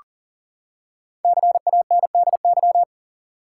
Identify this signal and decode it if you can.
Morse code, heard as a keyed tone, about 40 wpm, 700 Hz. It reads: CANDY
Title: Five Letter Words - 40wpm